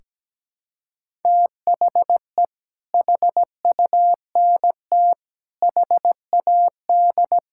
Morse code, heard as a keyed tone, about 17 wpm, 700 Hz. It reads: THE HUNT HAD
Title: Sentences from Top 1600 Words 17wpm